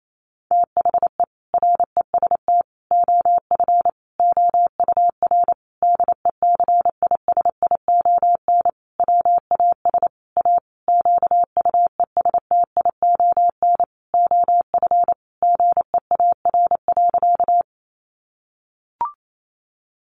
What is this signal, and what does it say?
Morse code, heard as a keyed tone, about 28 wpm, 700 Hz. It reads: THE REST OF OUR DECISION WAS A QUESTION OF GEAR.
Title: Sentences from Top 1400 Words 28wpm